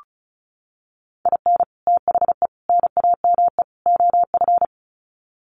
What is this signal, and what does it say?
Morse code, heard as a keyed tone, about 34 wpm, 700 Hz. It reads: IN THE NAME OF
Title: Sets of 4 Words 35wpm